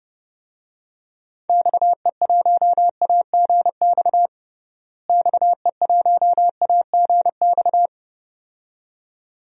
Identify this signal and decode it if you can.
Morse code, heard as a keyed tone, about 30 wpm, 700 Hz. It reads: XE1AGX XE1AGX
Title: DXCC-100 Call Signs - 6 Characters - Repeated 2x - 30wpm